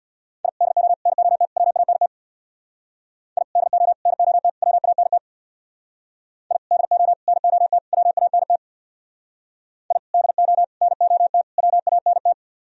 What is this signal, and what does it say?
Morse code, heard as a keyed tone, about 60 wpm, 700 Hz. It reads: I DO NOT WANT I DO NOT WANT I DO NOT WANT I DO NOT WANT
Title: Speed-Racing - Sets of 4 Words 50wpm